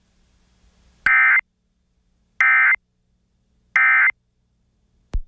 Emergency Alert System (SAME) tones, heard as harsh buzzing bursts. It is heard from 1.0 to 4.1 s.